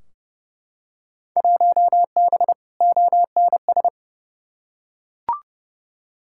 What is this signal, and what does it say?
Morse code, heard as a keyed tone, about 31 wpm, 700 Hz. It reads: 1B ONS
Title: Field Day Exchanges Encore - 30wpm